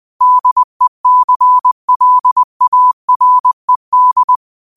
Morse code, heard as a keyed tone, about 20 wpm, 1000 Hz. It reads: DECLARED